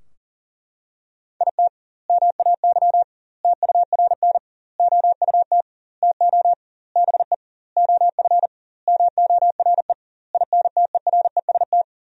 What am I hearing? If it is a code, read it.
Morse code, heard as a keyed tone, about 40 wpm, 700 Hz. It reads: IT MAY TURN OUT TO BE OF MORE INTEREST
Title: 2x Word-Spacing - Sentences from Top 400 Words - 40wpm